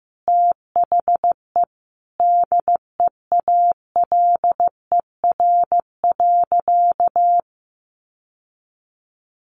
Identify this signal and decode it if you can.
Morse code, heard as a keyed tone, about 15 wpm, 700 Hz. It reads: THE DEALER.
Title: Sentences from Top 1100 Words 15wpm